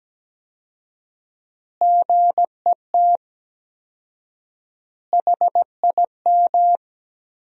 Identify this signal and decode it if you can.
Morse code, heard as a keyed tone, about 17 wpm, 700 Hz. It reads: GET HIM
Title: 4x Word-Spacing - Sentences from Top 400 Words - 17wpm